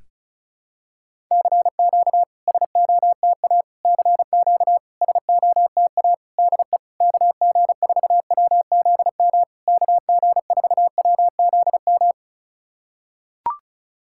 Morse code, heard as a keyed tone, about 35 wpm, 700 Hz. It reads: CQ SOTA CQ SOTA DE KG4WZM KG4WZM
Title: SOTA QSO Practice 35wpm